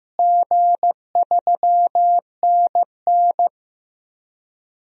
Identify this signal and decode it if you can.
Morse code, heard as a keyed tone, about 15 wpm, 700 Hz. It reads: G3NN
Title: DXCC-100 Call Signs - 4 Characters - Repeated 2x - 15wpm